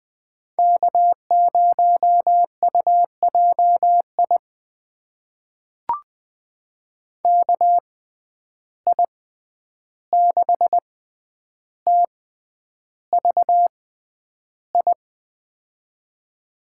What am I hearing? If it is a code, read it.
Morse code, heard as a keyed tone, about 20 wpm, spaced as for 10 wpm, 700 Hz. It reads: K0UJI KI6TVI
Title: USA Call Signs - 6x Character Spacing - 20wpm